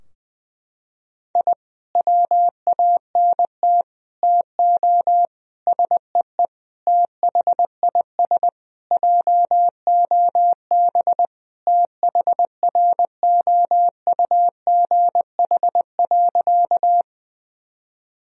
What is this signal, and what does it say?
Morse code, heard as a keyed tone, about 21 wpm, 700 Hz. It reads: I WANT TO SEE THIS JOB THROUGH.
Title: Sentences from Top 1000 Words - Review Part 3 - 20wpm